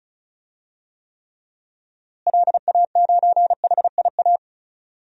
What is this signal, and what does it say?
Morse code, heard as a keyed tone, about 34 wpm, 700 Hz. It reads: LA9HIA
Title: DXCC-100 Call Signs - 6 Characters - Repeated 3x - 35wpm